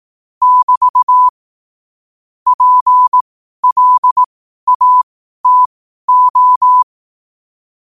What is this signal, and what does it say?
Morse code, heard as a keyed tone, about 18 wpm, spaced as for 12 wpm, 1000 Hz. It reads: <BT> PLATO
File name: qotdfiles / 12wpm